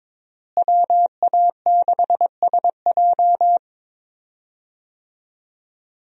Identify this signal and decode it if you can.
Morse code, heard as a keyed tone, about 22 wpm, 700 Hz. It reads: WA6SJ
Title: USA Call Signs - 5 Characters - Repeated 2x - 22wpm